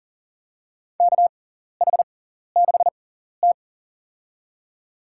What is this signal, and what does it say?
Morse code, heard as a keyed tone, about 40 wpm, spaced as for 14 wpm, 700 Hz. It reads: KH6T
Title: USA Call Signs - 4 Characters - 6x Character Spacing - 40wpm